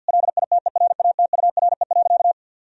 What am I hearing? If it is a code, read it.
Morse code, heard as a keyed tone, about 52 wpm, 700 Hz. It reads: LITERATURE.
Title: Sentences from Top 1000 Words - Review Part 3 - 50wpm